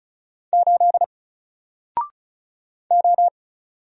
Morse code, heard as a keyed tone, about 36 wpm, 700 Hz. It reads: 8 O
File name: Single-Letter-Number-35wpm